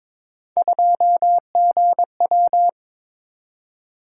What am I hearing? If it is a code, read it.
Morse code, heard as a keyed tone, about 22 wpm, 700 Hz. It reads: 2GW
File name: practice / 22wpm / Three Letter-Number Combinations - 22wpm